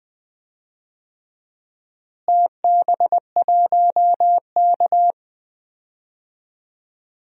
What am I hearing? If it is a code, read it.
Morse code, heard as a keyed tone, about 20 wpm, 700 Hz. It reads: TB1K
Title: DXCC-100 Call Signs - 4 Characters - Repeated 2x - 20wpm